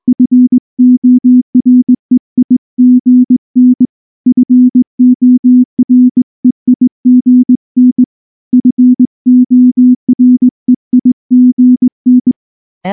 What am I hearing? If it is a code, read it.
Morse code, heard as a keyed tone, about 19 wpm, 250 Hz. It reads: FOREIGN FOREIGN FOREIGN
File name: morse / morse-training-20WPM-3